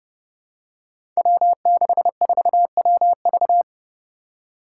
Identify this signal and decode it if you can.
Morse code, heard as a keyed tone, about 30 wpm, 700 Hz. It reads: W64WV
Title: Five Challenging Letter-Number Combinations 30wpm